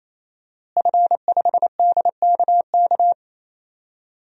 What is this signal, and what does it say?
Morse code, heard as a keyed tone, about 28 wpm, 700 Hz. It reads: F5DKK
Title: DXCC-100 Call Signs - 5 Characters - Repeated 3x - 28wpm